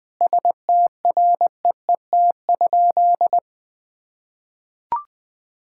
Morse code, heard as a keyed tone, about 21 wpm, 700 Hz. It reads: STREET?
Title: Sentences from Top 1000 Words - Review Part 6 - 20wpm